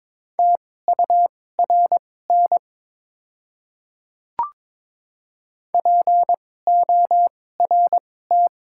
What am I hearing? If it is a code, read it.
Morse code, heard as a keyed tone, about 22 wpm, spaced as for 16 wpm, 700 Hz. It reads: TURN PORT
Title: 2x Character Spacing - Top 200 Words - 22wpm